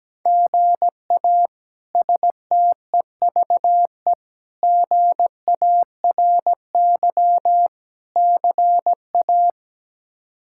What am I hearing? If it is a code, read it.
Morse code, heard as a keyed tone, about 17 wpm, 700 Hz. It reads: GA STEVE GARY CA
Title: K1USN SST QSO Practice - 17wpm